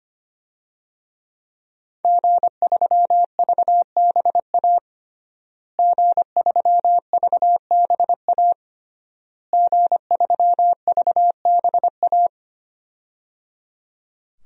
Morse code, heard as a keyed tone, about 25 wpm, 700 Hz. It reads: G3VBA G3VBA G3VBA
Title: DXCC-100 Call Signs - 5 Characters - Repeated 3x - 25wpm